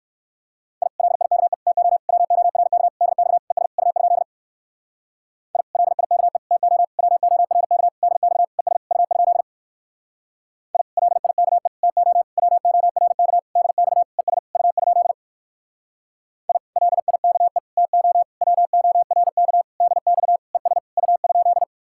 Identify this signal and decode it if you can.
Morse code, heard as a keyed tone, about 59 wpm, 700 Hz. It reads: I LIKE TO WORK DX ES U? I LIKE TO WORK DX ES U? I LIKE TO WORK DX ES U? I LIKE TO WORK DX ES U?